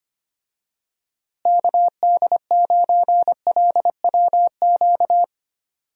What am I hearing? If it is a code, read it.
Morse code, heard as a keyed tone, about 25 wpm, 700 Hz. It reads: KD9LWQ
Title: USA Call Signs - 6 Characters - Repeated 3x - 25wpm